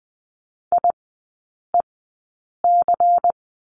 Morse code, heard as a keyed tone, about 20 wpm, spaced as for 8 wpm, 700 Hz. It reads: IEC
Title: 114 Lesson 19e - Letter w - Three Characters 20wpm